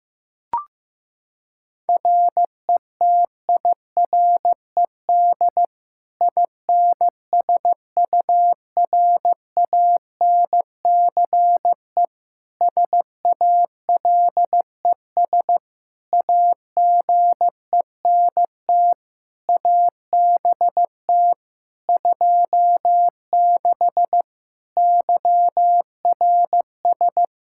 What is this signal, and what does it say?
Morse code, heard as a keyed tone, about 16 wpm, 700 Hz. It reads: RETIRED INSURANCE SALES AGENT ABT 26 YRS
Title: Practice QSO - Exchange 3 - More Info - 15wpm